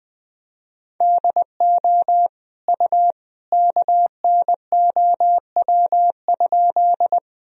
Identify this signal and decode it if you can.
Morse code, heard as a keyed tone, about 21 wpm, 700 Hz. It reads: DO U KNOW?
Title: Sentences from Top 100 Words and Abbreviations 20wpm